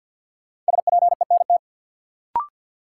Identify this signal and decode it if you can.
Morse code, heard as a keyed tone, about 53 wpm, 700 Hz. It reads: SPENT
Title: Sets of 5 Words - New Word Drill - 50wpm